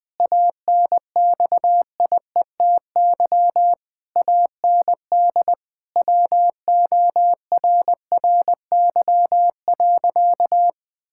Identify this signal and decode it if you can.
Morse code, heard as a keyed tone, about 20 wpm, 700 Hz. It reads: ANXIETY AND WORRY.
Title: Pangram Challenge v2 - 20wpm